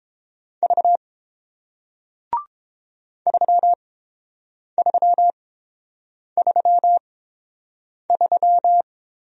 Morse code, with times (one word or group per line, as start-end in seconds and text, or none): 0.6-1.0 V
3.3-8.8 3333